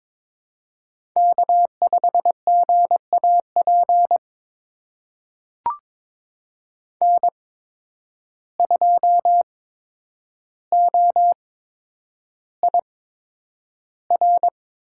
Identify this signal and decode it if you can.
Morse code, heard as a keyed tone, about 22 wpm, spaced as for 9 wpm, 700 Hz. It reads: K5GAP N2OIR
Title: USA Call Signs - 5 Characters - 8x Character Spacing - 22wpm